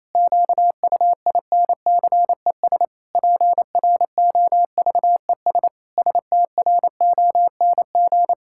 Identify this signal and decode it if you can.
Morse code, heard as a keyed tone, about 28 wpm, 700 Hz. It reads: QUINCES PROVES STRONG
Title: Pangram Challenge v2 - 28wpm